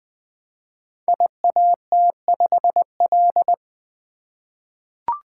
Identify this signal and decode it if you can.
Morse code, heard as a keyed tone, about 20 wpm, 700 Hz. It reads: IAT5L
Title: Code Groups - 5 Letter:Numbers - 20wpm